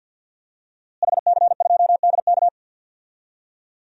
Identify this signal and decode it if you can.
Morse code, heard as a keyed tone, about 52 wpm, 700 Hz. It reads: HC2DK
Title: DXCC-100 Call Signs - 5 Characters - Repeated 3x - 50wpm